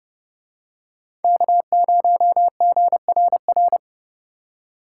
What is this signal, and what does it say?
Morse code, heard as a keyed tone, about 29 wpm, 700 Hz. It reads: K0GRR